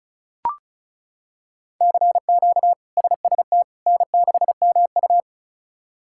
Morse code, heard as a keyed tone, about 34 wpm, 700 Hz. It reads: CQ SST N6MU
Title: K1USN SST QSO Practice - 35wpm